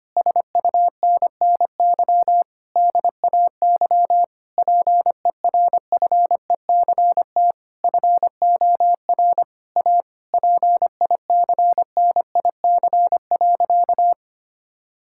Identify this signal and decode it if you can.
Morse code, heard as a keyed tone, about 25 wpm, 700 Hz. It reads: SUNNY DAY PERFECT FOR A PICNIC.
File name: practice / 25wpm / Difficult Sentences with Hints - 25wpm